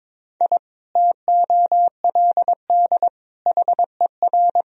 Morse code, heard as a keyed tone, about 22 wpm, 700 Hz. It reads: I TOLD HER
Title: Sentences from Top 1000 Words - Review Part 3 - 22wpm